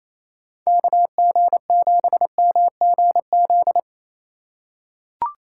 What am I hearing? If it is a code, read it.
Morse code, heard as a keyed tone, about 28 wpm, 700 Hz. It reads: KG7MGZ
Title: USA Call Signs - 6 Characters - 2x Character Spacing - 28wpm